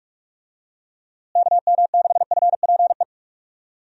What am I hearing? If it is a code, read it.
Morse code, heard as a keyed tone, about 43 wpm, 700 Hz. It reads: KM6FPE
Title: USA Call Signs - 6 Characters - Repeated 2x - 45wpm